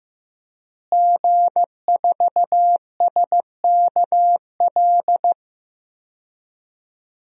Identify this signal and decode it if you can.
Morse code, heard as a keyed tone, about 15 wpm, 700 Hz. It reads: G4SKL
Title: DXCC-100 Call Signs - 5 Characters - Repeated 3x - 15wpm